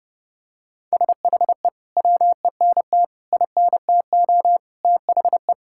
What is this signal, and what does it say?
Morse code, heard as a keyed tone, about 30 wpm, 700 Hz. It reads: SHE WENT INTO THE
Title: Sentences from Top 1600 Words 30wpm